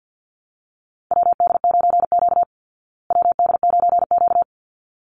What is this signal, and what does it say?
Morse code, heard as a keyed tone, about 50 wpm, 700 Hz. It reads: WD9Q WD9Q
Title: Call Signs Repeated 3x - 50wpm